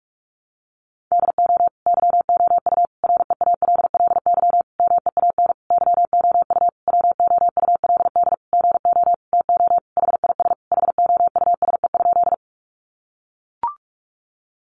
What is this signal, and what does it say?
Morse code, heard as a keyed tone, about 44 wpm, 700 Hz. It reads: DO YOU REALLY MEAN YOU WOULD GO TO HIS HOUSE?
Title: Sentences from Top 600 Words 45wpm